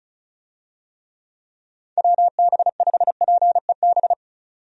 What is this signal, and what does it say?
Morse code, heard as a keyed tone, about 36 wpm, 700 Hz. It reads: WB5PEB
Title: USA Call Signs - 6 Characters - 35wpm